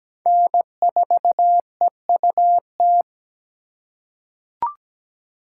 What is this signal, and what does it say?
Morse code, heard as a keyed tone, about 17 wpm, 700 Hz. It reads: N4EUT
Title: USA Call Signs - 5 Characters - 3x Character Spacing - 17wpm